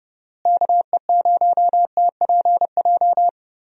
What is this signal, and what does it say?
Morse code, heard as a keyed tone, about 30 wpm, 700 Hz. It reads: KE0TPJ
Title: USA Call Signs - 6 Characters - Repeated 2x - 30wpm